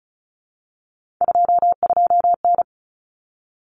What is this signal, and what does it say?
Morse code, heard as a keyed tone, about 34 wpm, 700 Hz. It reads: 22N
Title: Common Cut Numbers - 3-Digit 35wpm